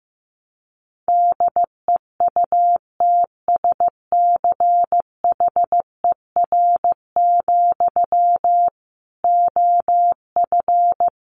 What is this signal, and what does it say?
Morse code, heard as a keyed tone, about 15 wpm, 700 Hz. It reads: DEUTSCHER, OF
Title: A Sound Of Thunder 15wpm